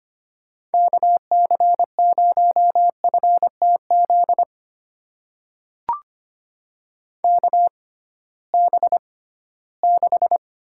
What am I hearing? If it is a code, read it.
Morse code, heard as a keyed tone, about 25 wpm, spaced as for 16 wpm, 700 Hz. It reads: KC0FTZ KB6
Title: USA Call Signs - 6 Characters - 6x Character Spacing - 25wpm